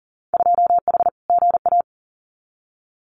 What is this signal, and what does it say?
Morse code, heard as a keyed tone, about 40 wpm, 700 Hz. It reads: 2H GA
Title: Winter Field Day Exchanges Encore 40wpm